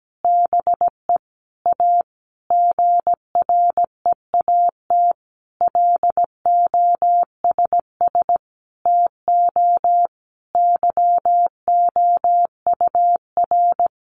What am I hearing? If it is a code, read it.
Morse code, heard as a keyed tone, about 17 wpm, 700 Hz. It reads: BE A GREAT LOSS TO YOUR
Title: Sentences from Top 1400 Words 17wpm